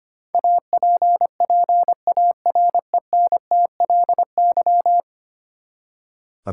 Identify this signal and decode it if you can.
Morse code, heard as a keyed tone, about 25 wpm, 700 Hz. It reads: APPARENTLY
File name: Ten Letter Words - 25wpm